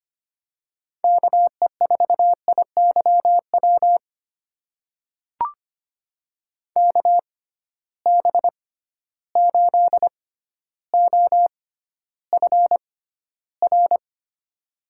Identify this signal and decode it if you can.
Morse code, heard as a keyed tone, about 25 wpm, spaced as for 13 wpm, 700 Hz. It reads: KE4IYW KB8OFR
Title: USA Call Signs - 6 Characters - 6x Character Spacing - 25wpm